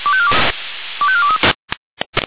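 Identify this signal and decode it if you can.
Five-tone selective call (ZVEI), heard as quick runs of five tones, tones of about 70 ms. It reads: 25423
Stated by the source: ZVEI five-tone selective call, received on SDR.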